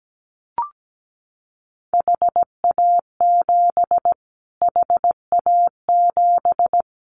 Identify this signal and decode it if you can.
Morse code, heard as a keyed tone, about 17 wpm, 700 Hz. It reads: HA7 HA7
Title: Callsign Prefixes for Top 25 DXCC - Repeated 3x - 17wpm